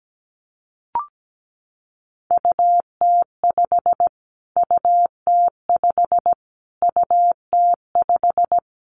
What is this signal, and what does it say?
Morse code, heard as a keyed tone, about 17 wpm, 700 Hz. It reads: UT5 UT5 UT5
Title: Callsign Prefixes for Top 25 DXCC - Repeated 3x - 17wpm